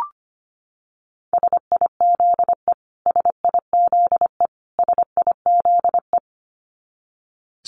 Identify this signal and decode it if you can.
Morse code, heard as a keyed tone, about 25 wpm, 700 Hz. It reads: SIZE SIZE SIZE
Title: Top 500 Words Repeated 25wpm